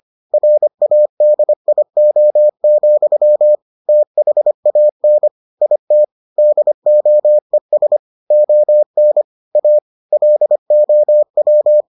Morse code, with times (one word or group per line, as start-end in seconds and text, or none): 0.3-3.5 RADIO,
3.9-5.3 THAN
5.6-6.0 IT
6.4-8.0 DOES
8.3-9.2 ON
9.6-9.8 A
10.1-11.8 LOW